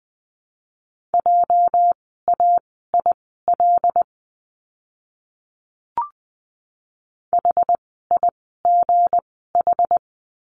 Morse code, heard as a keyed tone, about 20 wpm, spaced as for 14 wpm, 700 Hz. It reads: JAIL HIGH
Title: 206 Lesson 33f - Letter j - Words 20wpm